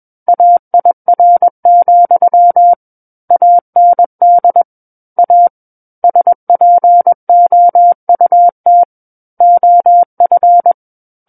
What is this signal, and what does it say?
Morse code, heard as a keyed tone, about 21 wpm, 700 Hz. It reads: AIR, AND A SPOUT OF
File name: Book One / 12-Chapter 12 - 21 WPM